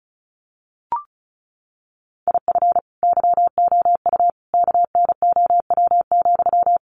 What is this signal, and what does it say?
Morse code, heard as a keyed tone, about 34 wpm, 700 Hz. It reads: IF YOU KNOW,
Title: Sentences from Top 300 Words 35wpm